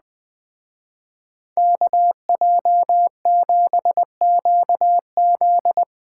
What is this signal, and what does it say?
Morse code, heard as a keyed tone, about 20 wpm, 700 Hz. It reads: KJ7QZ
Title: Speed Racing - USA 2x2 Call Signs 17wpm